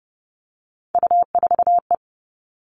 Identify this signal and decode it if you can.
Morse code, heard as a keyed tone, about 30 wpm, 700 Hz. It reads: U4E
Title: All Cut Numbers - 3-Digit 30wpm